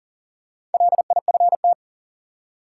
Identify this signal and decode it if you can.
Morse code, heard as a keyed tone, about 40 wpm, 700 Hz. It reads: LIFT
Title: Four Letter Words - 40wpm